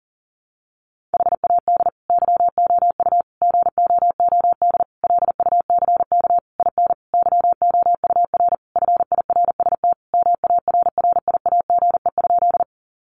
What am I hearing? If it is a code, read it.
Morse code, heard as a keyed tone, about 37 wpm, 700 Hz. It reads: HAD YOU GOOD LUCK IN YOUR FIRST MARRIAGE?
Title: Sentences from Top 1500 Words 40wpm